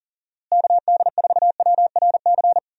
Morse code, heard as a keyed tone, about 40 wpm, 700 Hz. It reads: KD4WRC
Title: USA Call Signs - 6 Characters - Repeated 3x - 40wpm